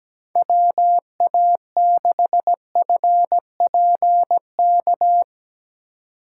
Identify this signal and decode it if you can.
Morse code, heard as a keyed tone, about 17 wpm, 700 Hz. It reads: WA6FPK
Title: USA Call Signs - 6 Characters - 3x Character Spacing - 17wpm